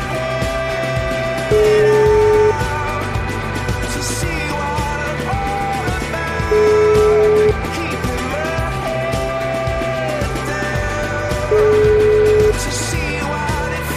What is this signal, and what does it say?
Ringback tone (Europe, 425 Hz), with noise.